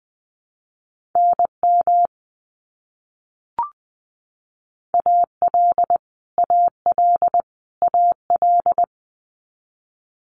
Morse code, heard as a keyed tone, about 20 wpm, 700 Hz. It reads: NM AL AL AL
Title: US States Repeated 3x 20wpm